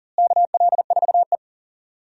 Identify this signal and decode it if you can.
Morse code, heard as a keyed tone, about 40 wpm, 700 Hz. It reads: KL4E
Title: USA Call Signs - 4 Characters - Repeated 2x - 40wpm